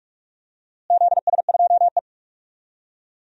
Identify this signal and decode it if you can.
Morse code, heard as a keyed tone, about 45 wpm, 700 Hz. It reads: ZS2E
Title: DXCC-100 Call Signs - 4 Characters - Repeated 3x - 45wpm